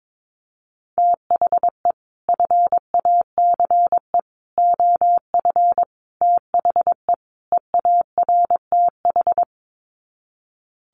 Morse code, heard as a keyed tone, about 22 wpm, 700 Hz. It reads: THE FACE OF THE EARTH
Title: Sets of 5 Words Encore 22wpm